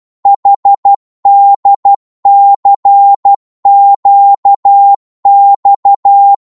Morse code, heard as a keyed tone, about 12 wpm, 800 Hz. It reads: HDCQX
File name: Morse / ltr-12wpm